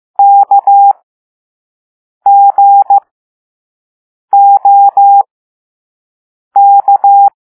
Morse code, heard as a keyed tone, about 15 wpm, spaced as for 5 wpm, 800 Hz. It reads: KGOK
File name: Morse / Level4A